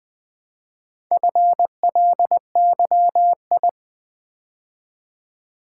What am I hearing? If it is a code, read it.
Morse code, heard as a keyed tone, about 20 wpm, 700 Hz. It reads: FLYI
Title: Build A Word - 2 Letters At A Time - 6-Letter Words - 20wpm